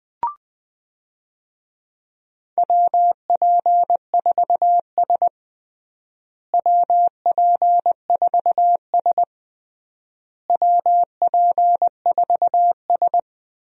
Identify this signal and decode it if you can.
Morse code, heard as a keyed tone, about 21 wpm, 700 Hz. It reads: WP4S WP4S WP4S
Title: USA Call Signs - 4 Characters - Repeated 3x - 20wpm